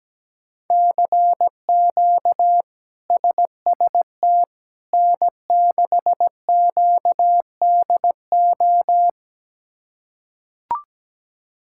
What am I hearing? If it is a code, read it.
Morse code, heard as a keyed tone, about 17 wpm, 700 Hz. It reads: CQ SST N6QDO
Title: K1USN SST QSO Practice - 17wpm